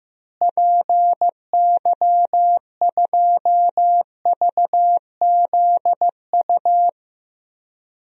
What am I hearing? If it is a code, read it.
Morse code, heard as a keyed tone, about 15 wpm, 700 Hz. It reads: PY2VZU
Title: DXCC-100 Call Signs - 6 Characters - Repeated 3x - 15wpm